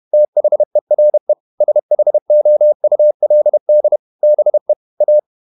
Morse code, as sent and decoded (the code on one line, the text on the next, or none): - .... .-. / ... .... --- ..- .-.. -.. / -... / .-
THERE SHOULD BE A